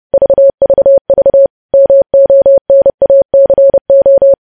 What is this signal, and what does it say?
Morse code, heard as a keyed tone, about 30 wpm, 550 Hz. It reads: VVV MONACO
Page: news-headlines-in-morse-code-at-30-wpm